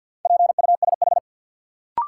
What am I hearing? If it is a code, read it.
Morse code, heard as a keyed tone, about 53 wpm, 700 Hz. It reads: PUSH
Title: Sentences from Top 1300 Words - New Word Drill Alt - 50wpm